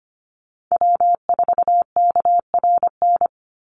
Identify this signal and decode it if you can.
Morse code, heard as a keyed tone, about 25 wpm, 700 Hz. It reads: W4KRN
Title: NA SOTA Chaser Callsigns 25wpm